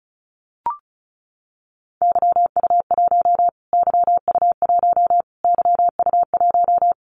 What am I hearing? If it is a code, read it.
Morse code, heard as a keyed tone, about 34 wpm, 700 Hz. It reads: YU1 YU1 YU1
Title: Callsign Prefixes for Top 25 DXCC - Repeated 3x - 35wpm